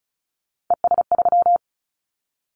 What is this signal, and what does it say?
Morse code, heard as a keyed tone, about 36 wpm, 700 Hz. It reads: ES3
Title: Callsign Prefixes for Top 100 DXCC - 35wpm